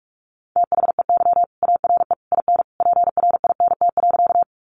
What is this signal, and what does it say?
Morse code, heard as a keyed tone, about 44 wpm, 700 Hz. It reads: THEY ARE IN PRINT.
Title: Sentences from Top 1400 Words 45wpm